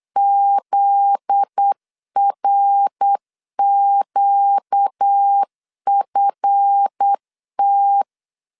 Morse code, heard as a keyed tone, about 8 wpm, 800 Hz. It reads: ZRQFT
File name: LEZIONI / 7 GF9Z / 7 GF9Z 8wpm 3213